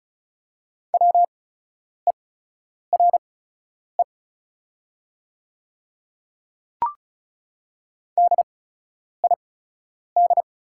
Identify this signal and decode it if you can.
Morse code, heard as a keyed tone, about 36 wpm, spaced as for 10 wpm, 700 Hz. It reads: WERE DID
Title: 8x Character Spacing - Top 500 Words - 35wpm